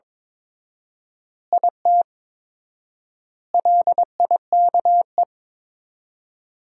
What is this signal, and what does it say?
Morse code, heard as a keyed tone, about 21 wpm, 700 Hz. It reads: IT LIKE
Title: 4x Word-Spacing - Sentences from Top 300 Words - 22wpm